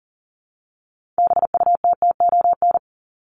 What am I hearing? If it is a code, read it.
Morse code, heard as a keyed tone, about 40 wpm, 700 Hz. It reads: BUTTON
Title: Top 1001-1250 Words 40wpm